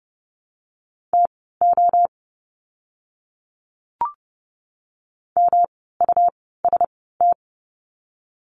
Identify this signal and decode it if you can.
Morse code, heard as a keyed tone, about 30 wpm, spaced as for 17 wpm, 700 Hz. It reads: TO MUST
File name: course / 30:10wpm / 108 Lesson 18f - Letter m - Words 30wpm